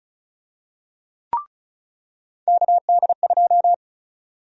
Morse code, heard as a keyed tone, about 36 wpm, 700 Hz. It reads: KD2
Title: USA Call Sign Prefixes - 35wpm